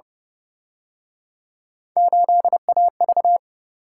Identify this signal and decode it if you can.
Morse code, heard as a keyed tone, about 31 wpm, 700 Hz. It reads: 8AV